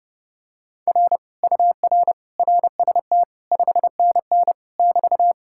Morse code, heard as a keyed tone, about 31 wpm, 700 Hz. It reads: R UR RST 5NN <BT>